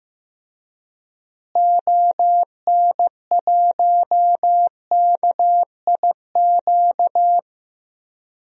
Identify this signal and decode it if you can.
Morse code, heard as a keyed tone, about 15 wpm, 700 Hz. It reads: ON1KIQ
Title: DXCC-100 Call Signs - 6 Characters - Repeated 2x - 15wpm